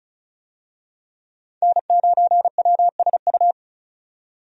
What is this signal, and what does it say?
Morse code, heard as a keyed tone, about 36 wpm, 700 Hz. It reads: N9WSU
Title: USA Call Signs - 5 Characters - 35wpm